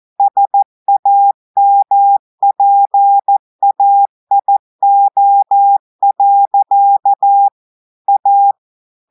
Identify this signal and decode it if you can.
Morse code, heard as a keyed tone, about 14 wpm, 800 Hz. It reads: SAMPAIO. A
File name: audios / portugais / portugais texte en clair 1 vitesse 14 mots minutes